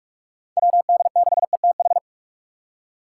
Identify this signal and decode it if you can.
Morse code, heard as a keyed tone, about 46 wpm, 700 Hz. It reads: WD6ETH
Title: USA Call Signs - 6 Characters - 4x Character Spacing - 45wpm